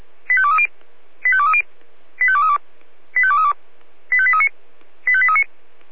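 Five-tone selective call (ZVEI), heard as quick runs of five tones, tones of about 70 ms. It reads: 85230 85230 85232 85232 85839 85839
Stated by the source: ZVEI five-tone selective call received on SDR.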